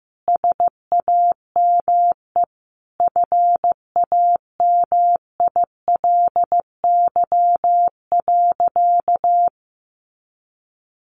Morse code, heard as a keyed tone, about 15 wpm, 700 Hz. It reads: SAME FAMILY.
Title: Sentences from Top 1500 Words 15wpm